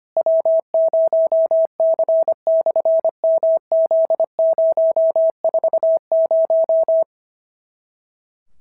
Morse code, heard as a keyed tone, about 25 wpm, 650 Hz. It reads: W0C/MZ040
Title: SOTA QSO Practice 25wpm